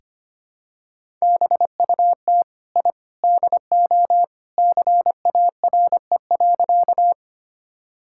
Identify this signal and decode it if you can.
Morse code, heard as a keyed tone, about 25 wpm, 700 Hz. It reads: BUT I DO CARE.
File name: practice / 25wpm / Sentences with 3-Word Phrases - 25wpm